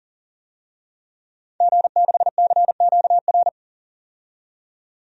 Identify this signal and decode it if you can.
Morse code, heard as a keyed tone, about 38 wpm, 700 Hz. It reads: G6CQR
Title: DXCC-100 Call Signs - 5 Characters - 40wpm